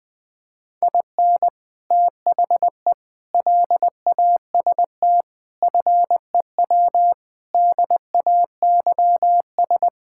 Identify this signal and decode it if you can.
Morse code, heard as a keyed tone, about 20 wpm, 700 Hz. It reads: IN THE LAST FEW DAYS
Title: Sentences with 3-Word Phrases - 20wpm